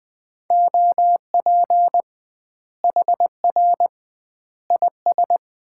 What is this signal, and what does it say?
Morse code, heard as a keyed tone, about 20 wpm, 700 Hz. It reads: OP HR IS
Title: Common QSO Phrases - 2x Word-Spacing - 20wpm